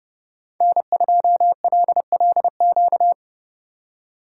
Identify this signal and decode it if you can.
Morse code, heard as a keyed tone, about 29 wpm, 700 Hz. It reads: N2LLQ